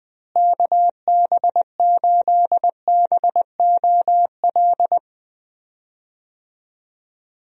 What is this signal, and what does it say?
Morse code, heard as a keyed tone, about 20 wpm, 700 Hz. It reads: KB8BOL
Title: USA Call Signs - 6 Characters - Repeated 2x - 20wpm